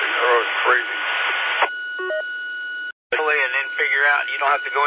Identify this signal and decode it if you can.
Emergency Alert System (SAME) tones, heard as harsh buzzing bursts, with speech. It is heard from 0.8 to 4.3 s.